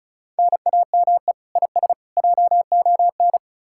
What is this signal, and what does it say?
Morse code, heard as a keyed tone, about 48 wpm, spaced as for 42 wpm, 700 Hz. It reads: NAME IS JON
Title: Speed-Racing Common QSO Phrases - 35wpm